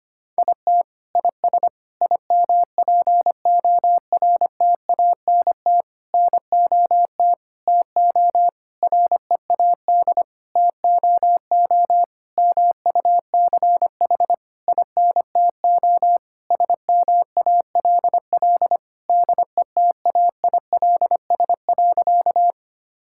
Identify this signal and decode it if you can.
Morse code, heard as a keyed tone, about 25 wpm, 700 Hz. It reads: IT IS IMPORTANT NOT TO READ TOO MUCH INTO SMALL DETAILS.
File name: practice / 25wpm / Sentences with 9-Word Phrases - 25wpm